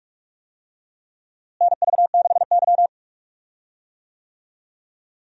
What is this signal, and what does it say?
Morse code, heard as a keyed tone, about 46 wpm, 700 Hz. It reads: NV6Y
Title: USA Call Signs - 4 Characters - Repeated 3x - 45wpm